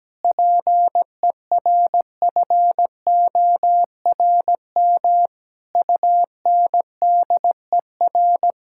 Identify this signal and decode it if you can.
Morse code, heard as a keyed tone, about 17 wpm, 700 Hz. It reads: PERFORM UNDER
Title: Sentences from Top 1000 Words - Review Part 6 - 17wpm